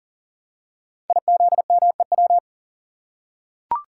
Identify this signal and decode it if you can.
Morse code, heard as a keyed tone, about 41 wpm, 690 Hz. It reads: IZMEW
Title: Code Groups - 5 Letters - 40wpm